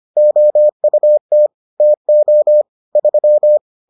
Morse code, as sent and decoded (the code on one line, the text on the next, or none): --- ..- - / - --- / ...--
OUT TO 3